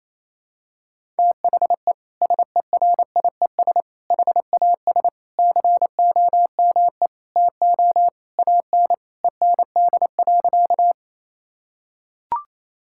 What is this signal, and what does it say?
Morse code, heard as a keyed tone, about 28 wpm, 700 Hz. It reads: THE SERIES HAS COME TO AN END.
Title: Sentences from Top 1000 Words - Review Part 5 - 28wpm